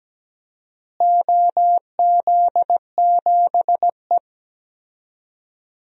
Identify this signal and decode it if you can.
Morse code, heard as a keyed tone, about 17 wpm, 700 Hz. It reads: OZ7E
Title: DXCC-100 Call Signs - 4 Characters - Repeated 3x - 17wpm